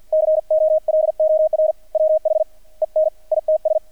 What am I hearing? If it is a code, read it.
Morse code, heard as a keyed tone, about 48 wpm, spaced as for 38 wpm, 620 Hz. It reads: COLOR WHEN ITS
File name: archived / CRSnet